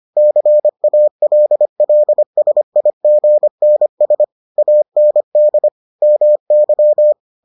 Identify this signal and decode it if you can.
Morse code, heard as a keyed tone, about 25 wpm, 600 Hz. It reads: CALLSIGNS AND MY